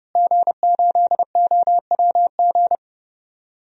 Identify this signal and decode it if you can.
Morse code, heard as a keyed tone, about 31 wpm, 700 Hz. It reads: G8OWG
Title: DXCC-100 Call Signs - 5 Characters - Repeated 3x - 30wpm